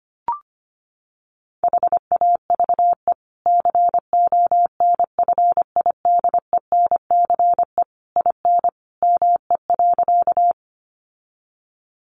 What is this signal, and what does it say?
Morse code, heard as a keyed tone, about 25 wpm, 700 Hz. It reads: HAVE CONFIDENCE IN ME.
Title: Sentences from Top 1500 Words 25wpm